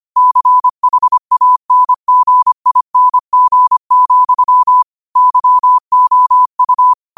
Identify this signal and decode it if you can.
Morse code, heard as a keyed tone, about 25 wpm, 1000 Hz. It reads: CHANGING, YOU
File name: qotdfiles / 25wpm